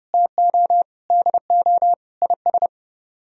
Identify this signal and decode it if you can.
Morse code, heard as a keyed tone, about 37 wpm, 700 Hz. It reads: TO DO IS
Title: Speed-Racing - Sets of 5 Words 30wpm